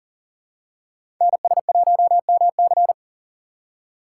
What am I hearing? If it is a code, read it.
Morse code, heard as a keyed tone, about 38 wpm, 700 Hz. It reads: NS1MC
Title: USA Call Signs - 5 Characters - 6x Character Spacing - 40wpm